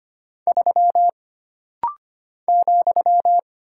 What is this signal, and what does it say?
Morse code, heard as a keyed tone, about 25 wpm, 700 Hz. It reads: 3 ,
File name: Single Character (Letter:Number:Punctuation:Prosign) - Rapid-Fire - 25wpm